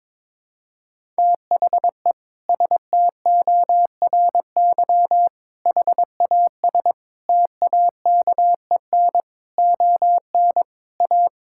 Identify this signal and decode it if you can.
Morse code, heard as a keyed tone, about 22 wpm, 700 Hz. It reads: THE STORY HAS TAKEN ON A
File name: practice / 22wpm / Sentences with 8-Word Phrases - 22wpm